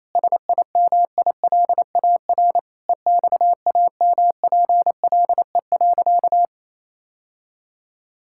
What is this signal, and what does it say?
Morse code, heard as a keyed tone, about 28 wpm, 700 Hz. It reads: SIMILAR EXAMPLE.
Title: Sentences from Top 1000 Words - Review Part 5 - 28wpm